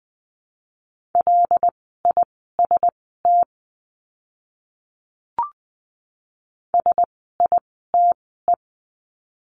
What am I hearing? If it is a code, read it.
Morse code, heard as a keyed tone, about 20 wpm, spaced as for 15 wpm, 700 Hz. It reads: LIST SITE
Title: 075 Lesson 13f - Letter L - Words 20wpm